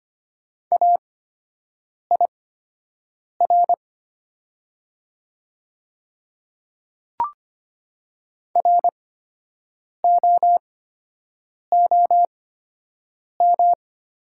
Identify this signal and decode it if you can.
Morse code, heard as a keyed tone, about 25 wpm, spaced as for 7 wpm, 700 Hz. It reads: AIR ROOM
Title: 8x Character Spacing - Top 300 Words - 25wpm